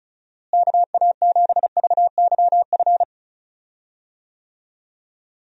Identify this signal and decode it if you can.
Morse code, heard as a keyed tone, about 36 wpm, 700 Hz. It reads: KA7VYF
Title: USA Call Signs - 6 Characters - Repeated 3x - 35wpm